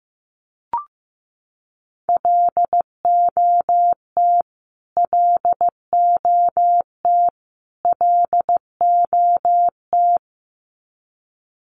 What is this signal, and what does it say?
Morse code, heard as a keyed tone, about 15 wpm, 700 Hz. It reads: LOT LOT LOT
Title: Top 500 Words Repeated 15wpm